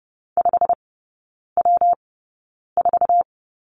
Morse code, heard as a keyed tone, about 30 wpm, spaced as for 9 wpm, 700 Hz. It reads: 5W4
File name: Lesson 19e - Letter w - Three Characters 30wpm